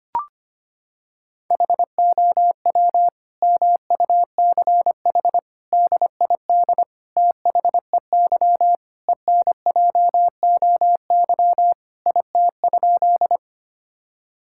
Morse code, heard as a keyed tone, about 25 wpm, 700 Hz. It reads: HOW MUCH DID THEY ENJOY IT?
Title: Sentences from Top 1000 Words - Review Part 5 - 25wpm